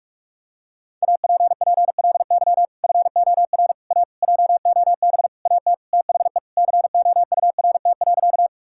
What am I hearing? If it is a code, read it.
Morse code, heard as a keyed tone, about 42 wpm, 700 Hz. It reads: APPLY FOR A JOB AT THE COURT.
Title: Sentences from Top 1600 Words 45wpm